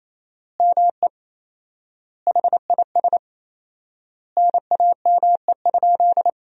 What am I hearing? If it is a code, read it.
Morse code, heard as a keyed tone, about 28 wpm, 700 Hz. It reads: ME HIS NAME?
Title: 4x Word-Spacing - Sentences from Top 400 Words - 28wpm